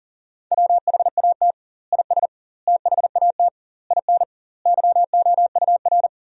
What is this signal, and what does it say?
Morse code, heard as a keyed tone, about 37 wpm, 700 Hz. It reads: WHAT IS THAT IN YOUR
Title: 2x Word-Spacing - Sentences from Top 200 Words - 40wpm